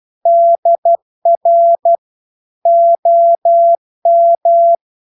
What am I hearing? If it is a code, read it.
Morse code, heard as a keyed tone, about 12 wpm, 680 Hz. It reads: DR OM